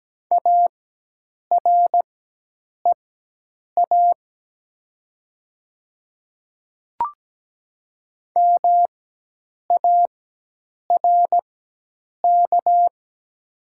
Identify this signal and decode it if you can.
Morse code, heard as a keyed tone, about 17 wpm, spaced as for 8 wpm, 690 Hz. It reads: AREA MARK
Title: 4x Character Spacing - Top 500 Words - 17wpm